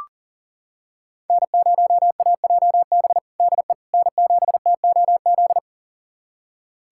Morse code, heard as a keyed tone, about 37 wpm, 700 Hz. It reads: N0AJB DE N7TOZ
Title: USA Callsign DE Callsign - 5 Characters - 40wpm